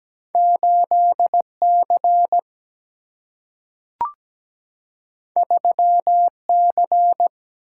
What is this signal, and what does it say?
Morse code, heard as a keyed tone, about 17 wpm, 700 Hz. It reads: 8C 3C
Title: Single-Letter Single-Number 17wpm